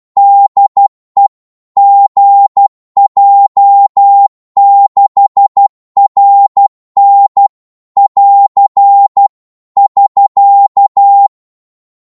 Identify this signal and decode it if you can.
Morse code, heard as a keyed tone, about 12 wpm, 800 Hz. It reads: DE GJ6RN <AR> <SK>
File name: Morse / qso-12wpm